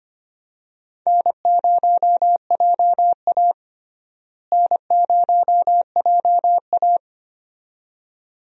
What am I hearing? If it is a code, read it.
Morse code, heard as a keyed tone, about 25 wpm, 700 Hz. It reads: N0JA N0JA
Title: USA Call Signs - 4 Characters - Repeated 2x - 25wpm